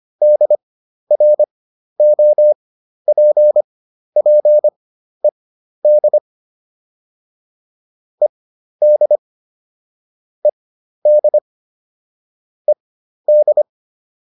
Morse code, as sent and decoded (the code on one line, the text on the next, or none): -.. .-. --- .--. .--. -.. / . -.. / . -.. / . -..
DROPPED ED ED ED